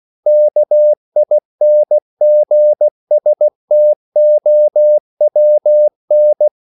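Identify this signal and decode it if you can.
Morse code, heard as a keyed tone, about 16 wpm, 600 Hz. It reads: KINGSTOWN
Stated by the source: CW practice of national capitals at 16 words per minute, from ON6ZQ.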